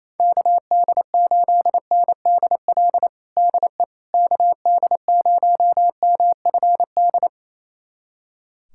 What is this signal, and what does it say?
Morse code, heard as a keyed tone, about 28 wpm, 700 Hz. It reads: KD8NDL DE KD0MFD
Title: USA Callsign DE Callsign 28wpm